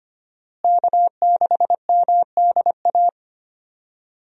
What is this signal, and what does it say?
Morse code, heard as a keyed tone, about 25 wpm, 700 Hz. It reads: K6MDA
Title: USA Call Signs - 5 Characters - Repeated 3x - 25wpm